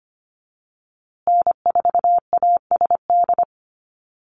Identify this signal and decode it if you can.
Morse code, heard as a keyed tone, about 25 wpm, 700 Hz. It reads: N4ASD
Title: USA Call Signs Repeated 2x 25wpm